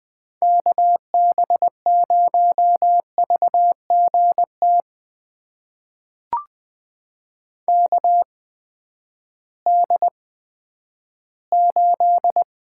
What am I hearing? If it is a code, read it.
Morse code, heard as a keyed tone, about 21 wpm, spaced as for 12 wpm, 700 Hz. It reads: KB0VGT KD8
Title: USA Call Signs - 6 Characters - 8x Character Spacing - 20wpm